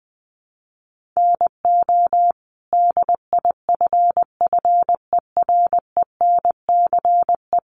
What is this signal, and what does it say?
Morse code, heard as a keyed tone, about 20 wpm, 700 Hz. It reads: NO DIFFERENCE
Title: Sentences from Top 800 Words 20wpm